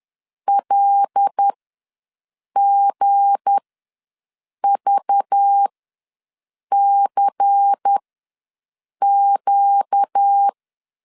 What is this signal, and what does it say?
Morse code, heard as a keyed tone, about 10 wpm, spaced as for 6 wpm, 800 Hz. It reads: LGVCQ
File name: LEZIONI / 7 GF9Z / 7 G ws 10wpm 3843